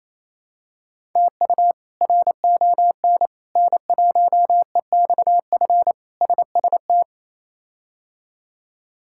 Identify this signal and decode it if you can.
Morse code, heard as a keyed tone, about 28 wpm, 700 Hz. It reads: TU RON N1EXF SST
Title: K1USN SST QSO Practice - 28wpm